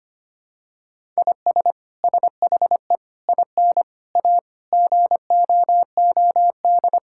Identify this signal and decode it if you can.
Morse code, heard as a keyed tone, about 25 wpm, 700 Hz. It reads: IS SHE IN A GOOD
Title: Sentences from Top 1000 Words - Review Part 6 - 25wpm